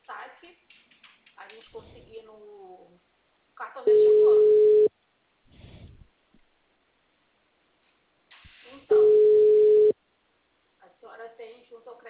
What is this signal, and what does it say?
Ringback tone (Europe, 425 Hz), with speech.